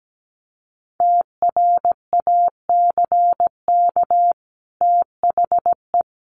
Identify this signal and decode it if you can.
Morse code, heard as a keyed tone, about 17 wpm, 700 Hz. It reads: TRACK THE
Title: Sentences from Top 1100 Words 17wpm